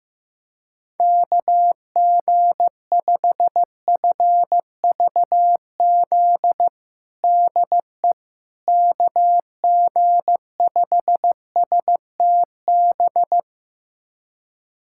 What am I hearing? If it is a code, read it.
Morse code, heard as a keyed tone, about 15 wpm, 700 Hz. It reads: KG5FVZ DE KG5STB
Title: USA Callsign DE Callsign - 6 Characters - 15wpm